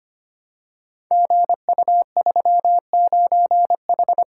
Morse code, heard as a keyed tone, about 25 wpm, 700 Hz. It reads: GU39H
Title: Code Groups - 5 Letter:Numbers - 25wpm